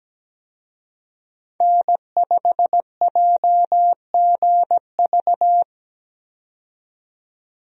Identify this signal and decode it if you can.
Morse code, heard as a keyed tone, about 17 wpm, 700 Hz. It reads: N5JGV
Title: USA Call Signs - 5 Characters - 17wpm